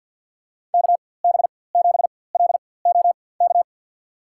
Morse code, heard as a keyed tone, about 52 wpm, spaced as for 23 wpm, 700 Hz. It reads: KB7LOK
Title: USA Call Signs - 4x Character Spacing - 50wpm